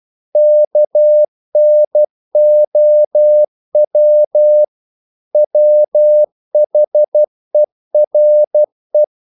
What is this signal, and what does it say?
Morse code, heard as a keyed tone, about 12 wpm, 600 Hz. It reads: KNOW WHERE
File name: CW / Practice / EnglishQuotations0312